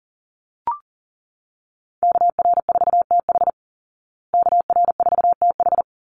Morse code, heard as a keyed tone, about 40 wpm, 700 Hz. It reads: KR4TH KR4TH
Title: Call Signs Repeated 3x - 40wpm